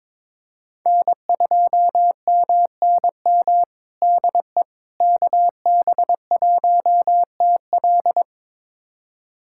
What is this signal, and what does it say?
Morse code, heard as a keyed tone, about 22 wpm, 700 Hz. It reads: N2MNM DE KB1TL
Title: USA Callsign DE Callsign - 5 Characters - 22wpm